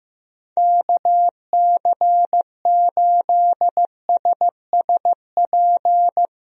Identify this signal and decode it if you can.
Morse code, heard as a keyed tone, about 15 wpm, 700 Hz. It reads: KC8SSP
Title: USA Call Signs - 6 Characters - 15wpm